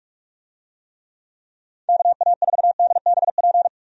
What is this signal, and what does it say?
Morse code, heard as a keyed tone, about 43 wpm, 700 Hz. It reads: KA4DBP
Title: USA Call Signs - 6 Characters - 45wpm